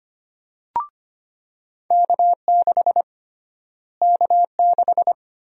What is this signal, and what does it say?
Morse code, heard as a keyed tone, about 25 wpm, 700 Hz. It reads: K6 K6
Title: USA Call Sign Prefixes - Repeated 2x - 25wpm